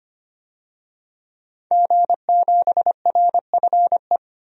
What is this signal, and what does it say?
Morse code, heard as a keyed tone, about 25 wpm, 700 Hz. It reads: G7RFE
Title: DXCC-100 Call Signs - 5 Characters - 25wpm